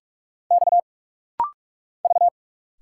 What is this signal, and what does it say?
Morse code, heard as a keyed tone, about 46 wpm, 700 Hz. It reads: X V
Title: Single Letters - Rapid-Fire 45wpm